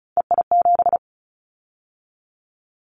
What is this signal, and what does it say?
Morse code, heard as a keyed tone, about 36 wpm, 700 Hz. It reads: EI7
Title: Callsign Prefixes for Top 50 DXCC - 35wpm